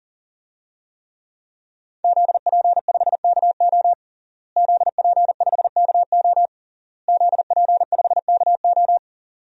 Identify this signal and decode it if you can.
Morse code, heard as a keyed tone, about 40 wpm, 700 Hz. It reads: ZP5KO ZP5KO ZP5KO
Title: DXCC-100 Call Signs - 5 Characters - Repeated 3x - 40wpm